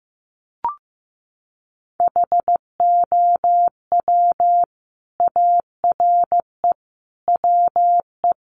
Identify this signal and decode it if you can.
Morse code, heard as a keyed tone, about 15 wpm, 700 Hz. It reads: HOW ARE WE
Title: Sets of 5 Words Encore 15wpm